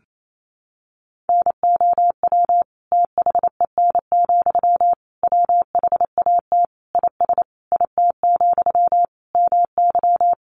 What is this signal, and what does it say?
Morse code, heard as a keyed tone, about 28 wpm, 700 Hz. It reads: NOW THEN, WHAT IS IT, MY
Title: Sentences from Top 1100 Words 28wpm